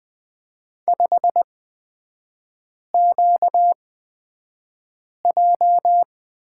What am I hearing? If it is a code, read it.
Morse code, heard as a keyed tone, about 20 wpm, spaced as for 5 wpm, 700 Hz. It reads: 5QJ
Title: 236 Lesson 38e - Letter q - Three Characters 20wpm